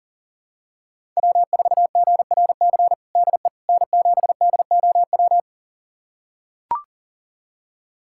Morse code, heard as a keyed tone, about 40 wpm, 700 Hz. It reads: W4GRC DE N7DOW
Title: USA Callsign DE Callsign - 5 Characters - 40wpm